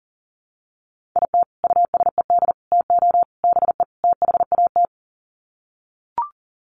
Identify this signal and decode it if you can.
Morse code, heard as a keyed tone, about 40 wpm, 700 Hz. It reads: IT USED TO BE THAT
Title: Sets of 5 Words Encore2 40wpm